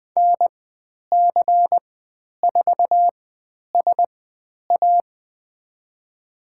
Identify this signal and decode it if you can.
Morse code, heard as a keyed tone, about 20 wpm, spaced as for 10 wpm, 700 Hz. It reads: NC4SA
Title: 109 Lesson 18g - Letter m - Call Signs 20wpm